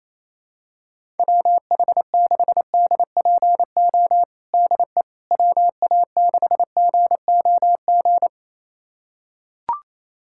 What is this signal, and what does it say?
Morse code, heard as a keyed tone, about 28 wpm, 700 Hz. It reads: WH6DPO DE WA6GOG
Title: USA Callsign DE Callsign 28wpm